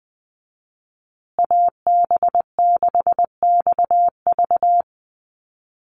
Morse code, heard as a keyed tone, about 20 wpm, 700 Hz. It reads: AB6XV
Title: USA Call Signs Repeated 2x 20wpm